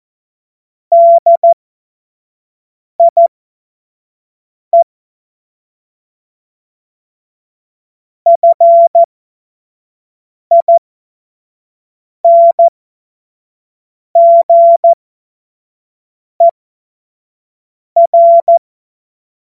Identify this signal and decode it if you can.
Morse code, heard as a keyed tone, about 14 wpm, spaced as for 5 wpm, 680 Hz. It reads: DIE FINGER
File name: Morsekurs / Witze / Witz 04. Januar 2022, 25 F70